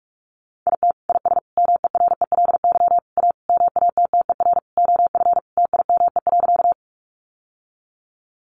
Morse code, heard as a keyed tone, about 44 wpm, 700 Hz. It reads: IT IS MERELY A MATTER OF TIME.
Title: Sentences from Top 1400 Words 45wpm